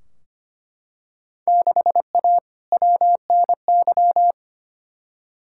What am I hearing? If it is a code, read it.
Morse code, heard as a keyed tone, about 25 wpm, 700 Hz. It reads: 6A WNY
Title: Field Day Exchanges Encore - 25wpm